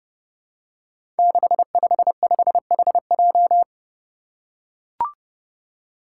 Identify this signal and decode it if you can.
Morse code, heard as a keyed tone, about 30 wpm, 700 Hz. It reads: 655HJ
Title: Five Challenging Letter-Number Combinations 30wpm